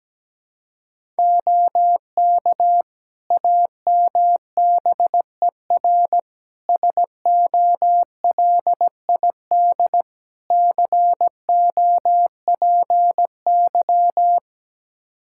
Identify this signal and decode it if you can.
Morse code, heard as a keyed tone, about 17 wpm, 700 Hz. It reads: OK AMBER SOLID COPY
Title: Practice QSO - Exchange 3 - More Info - 17wpm